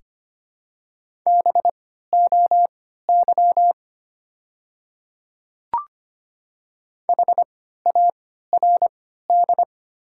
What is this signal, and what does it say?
Morse code, heard as a keyed tone, about 25 wpm, spaced as for 14 wpm, 700 Hz. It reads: BOY HARD
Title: 3x Character Spacing - Top 300 Words - 25wpm